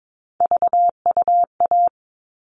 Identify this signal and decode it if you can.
Morse code, heard as a keyed tone, about 22 wpm, 700 Hz. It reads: VUA
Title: All Cut Numbers - 3-Digit 22wpm